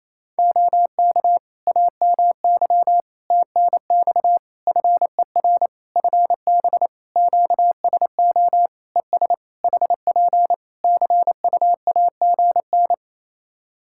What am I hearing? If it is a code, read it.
Morse code, heard as a keyed tone, about 27 wpm, 700 Hz. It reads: OK AMY TNX FER FB QSO ES HP CUAGN
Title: Practice QSO - Exchange 4 - Ending - 28wpm